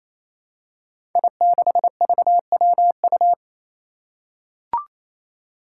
Morse code, heard as a keyed tone, about 27 wpm, 700 Hz. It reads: I6VWU
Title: Five Challenging Letter-Number Combinations 28wpm